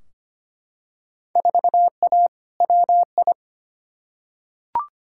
Morse code, heard as a keyed tone, about 25 wpm, 700 Hz. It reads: 4A WI
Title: Field Day Exchanges - 25wpm